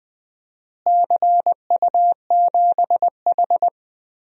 Morse code, heard as a keyed tone, about 20 wpm, 700 Hz. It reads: CU7H
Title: DXCC-100 Call Signs - 4 Characters - Repeated 3x - 20wpm